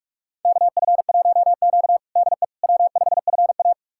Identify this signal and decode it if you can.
Morse code, heard as a keyed tone, about 46 wpm, 700 Hz. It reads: KF1Q DE W5FA
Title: USA Callsign DE Callsign - 4 Characters - 45wpm